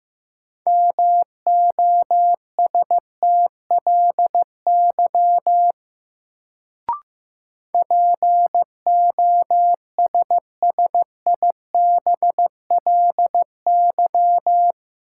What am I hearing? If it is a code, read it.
Morse code, heard as a keyed tone, about 15 wpm, 700 Hz. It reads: MOSTLY POSSIBLY
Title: Sentences from Top 1300 Words - New Word Drill - 15wpm